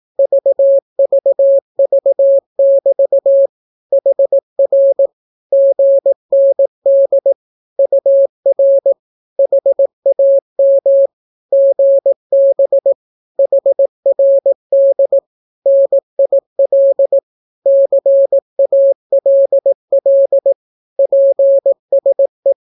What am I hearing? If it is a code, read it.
Morse code, heard as a keyed tone, about 18 wpm, 550 Hz. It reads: VVV<BT> HR GND UR HAM GB HRD NIL CALL PSE
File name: naukapodstaw / skroty / skroty